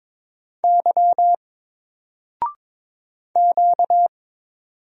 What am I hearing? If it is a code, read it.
Morse code, heard as a keyed tone, about 22 wpm, 700 Hz. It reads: Y Q
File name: Single-Letter-Number-22wpm